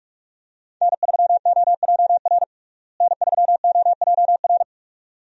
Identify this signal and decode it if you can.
Morse code, heard as a keyed tone, about 41 wpm, 700 Hz. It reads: N3OJR N3OJR
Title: USA Call Signs - 5 Characters - Repeated 2x - 45wpm